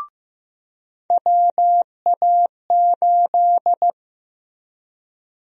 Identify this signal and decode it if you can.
Morse code, heard as a keyed tone, about 15 wpm, 700 Hz. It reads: WA8
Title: USA Call Sign Prefixes - Repeated 2x - 15wpm